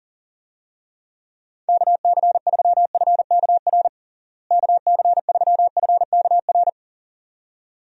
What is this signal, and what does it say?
Morse code, heard as a keyed tone, about 40 wpm, 700 Hz. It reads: KC3FKR KC3FKR
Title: USA Call Signs - 6 Characters - Repeated 2x - 40wpm